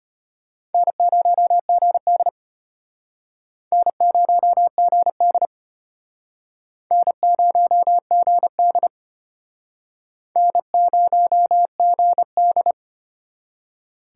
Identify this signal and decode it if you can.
Morse code, heard as a keyed tone, about 31 wpm, 700 Hz. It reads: N0GD N0GD N0GD N0GD